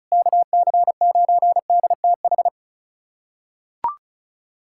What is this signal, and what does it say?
Morse code, heard as a keyed tone, about 34 wpm, 700 Hz. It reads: KC9DTH
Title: USA Callsign DE Callsign - 6 Characters - 35wpm